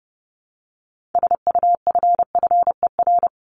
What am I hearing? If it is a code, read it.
Morse code, heard as a keyed tone, about 30 wpm, 700 Hz. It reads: SUFFER
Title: Top 1501-1750 Words 30wpm